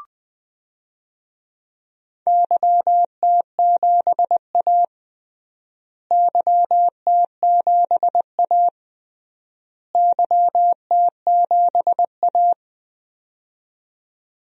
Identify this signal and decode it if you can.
Morse code, heard as a keyed tone, about 20 wpm, 700 Hz. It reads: YT7A YT7A YT7A
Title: DXCC-100 Call Signs - 4 Characters - Repeated 3x - 20wpm